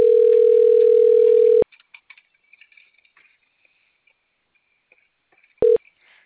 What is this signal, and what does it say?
Ringback tone (North America), 440 + 480 Hz.